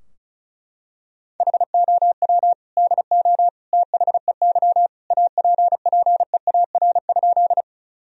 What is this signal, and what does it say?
Morse code, heard as a keyed tone, about 36 wpm, 700 Hz. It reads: HOW DO THEY APPEAR?
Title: Sentences from Top 1000 Words - Review Part 2 - 35wpm